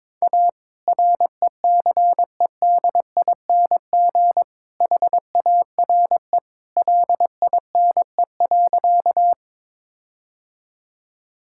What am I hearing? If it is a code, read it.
Morse code, heard as a keyed tone, about 22 wpm, 690 Hz. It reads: A RECEDING HARE LINE.
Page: morse-code-ninja-practice-22wpm